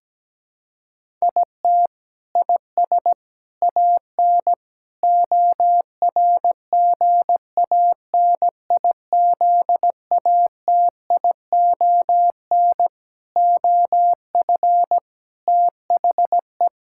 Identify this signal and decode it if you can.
Morse code, heard as a keyed tone, about 17 wpm, 700 Hz. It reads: IT IS AN ORGANIZATION OF THE
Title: Sentences from Top 1000 Words - Review Part 6 - 17wpm